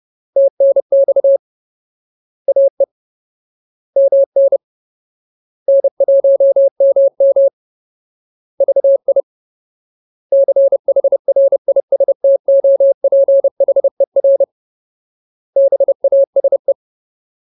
Morse code, heard as a keyed tone, about 30 wpm, 550 Hz. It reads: TNX AE MN N1MM VI CHRISTOPHER BASE
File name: Ultimate Word List / 30WPM Ultimate